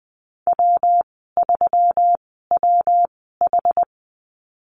Morse code, heard as a keyed tone, about 20 wpm, spaced as for 14 wpm, 700 Hz. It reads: W3WH
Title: 140 Lesson 23g - Letter f - Call Signs 20wpm